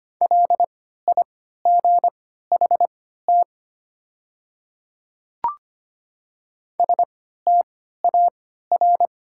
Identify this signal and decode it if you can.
Morse code, heard as a keyed tone, about 25 wpm, spaced as for 14 wpm, 700 Hz. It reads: LIGHT STAR
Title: 3x Character Spacing - Top 500 Words - 25wpm